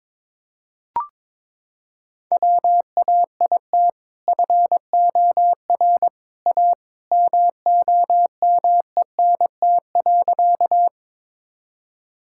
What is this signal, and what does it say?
Morse code, heard as a keyed tone, about 22 wpm, 700 Hz. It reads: WAIT FOR A MOMENT.
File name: practice / 22wpm / Sentences with 2-Word Phrases - 22wpm